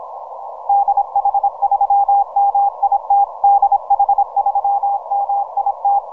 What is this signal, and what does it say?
Morse code, heard as a keyed tone, about 27 wpm, 800 Hz. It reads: DH3MIT DH3MIT
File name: VK0EK 12MCW